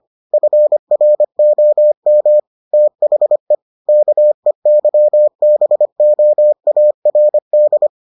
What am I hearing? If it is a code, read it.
Morse code, heard as a keyed tone, about 25 wpm, 600 Hz. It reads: FROM THE KEYBOARD